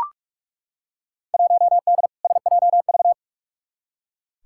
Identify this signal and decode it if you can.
Morse code, heard as a keyed tone, about 43 wpm, 700 Hz. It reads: 1D SJV
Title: Field Day Exchanges Encore - 45wpm